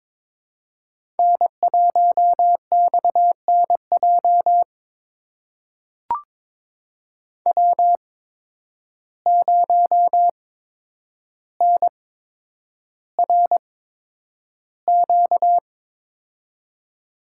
Morse code, heard as a keyed tone, about 23 wpm, spaced as for 10 wpm, 700 Hz. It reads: N1XNJ W0NRQ
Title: USA Call Signs - 5 Characters - 8x Character Spacing - 22wpm